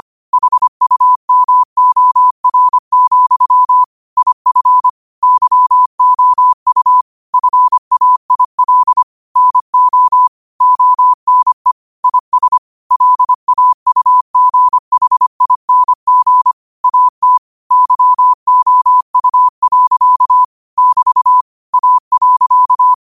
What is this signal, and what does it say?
Morse code, heard as a keyed tone, about 25 wpm, 1000 Hz. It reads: HUMOR, IF YOU FAIL NO ONE IS LAUGHING AT YOU. <BT> A.
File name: qotdfiles / 25wpm